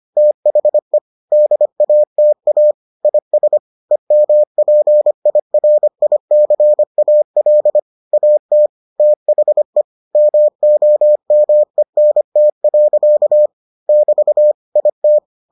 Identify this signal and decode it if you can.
Morse code, heard as a keyed tone, about 25 wpm, 600 Hz. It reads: THE DATA IS EMPIRICAL AT THE MOMENT. <BT> IT